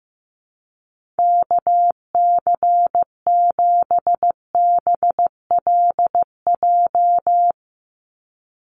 Morse code, heard as a keyed tone, about 15 wpm, 700 Hz. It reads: KC7BLJ
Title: USA Call Signs 15wpm